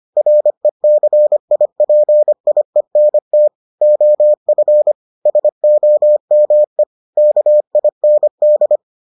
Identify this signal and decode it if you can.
Morse code, heard as a keyed tone, about 25 wpm, 600 Hz. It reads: RECIPIENT OF SOME KIND